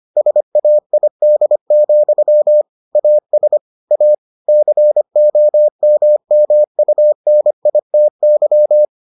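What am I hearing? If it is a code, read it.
Morse code, heard as a keyed tone, about 25 wpm, 600 Hz. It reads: SAID, AS A COMMUNITY
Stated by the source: Morse edition of the Foundations of Amateur Radio episodes.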